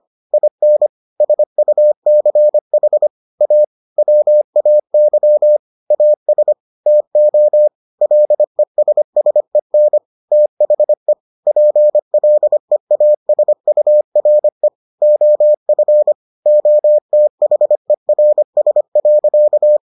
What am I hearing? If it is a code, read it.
Morse code, heard as a keyed tone, about 25 wpm, 600 Hz. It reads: IN SUCH A WAY AS TO LESSEN THE PLEASURE OF OTHERS.